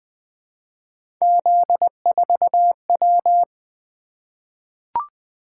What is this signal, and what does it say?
Morse code, heard as a keyed tone, about 20 wpm, 700 Hz. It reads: Z4W
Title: Three Letter-Number Combinations - 20wpm